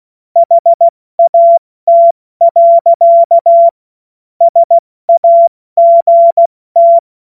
Morse code, heard as a keyed tone, about 16 wpm, 680 Hz. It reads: HAT. SAGT
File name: Morsekurs / Witze / Witz 01. Februar 2022 70 F80